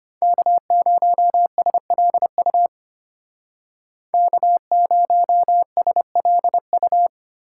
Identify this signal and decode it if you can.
Morse code, heard as a keyed tone, about 30 wpm, 700 Hz. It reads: K0SLU K0SLU
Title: Speed Racing - USA Call Signs 25wpm